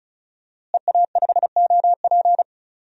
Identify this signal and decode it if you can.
Morse code, heard as a keyed tone, about 34 wpm, 700 Hz. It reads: EA5OP
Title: DXCC-100 Call Signs - 5 Characters - Repeated 2x - 35wpm